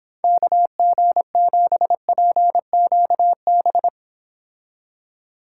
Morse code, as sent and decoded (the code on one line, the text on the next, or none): -.- --. --... .--. --.- -...
KG7PQB